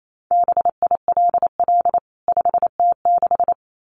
Morse code, heard as a keyed tone, about 28 wpm, 700 Hz. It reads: BILL 5T6
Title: CWT Exchange 28wpm